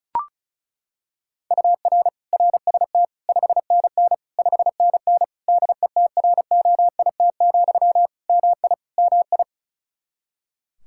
Morse code, heard as a keyed tone, about 34 wpm, 700 Hz. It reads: UR RST 5NN 5NN DETROIT, MI MI
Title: Practice QSO - Exchange 1 - Beginning - 35wpm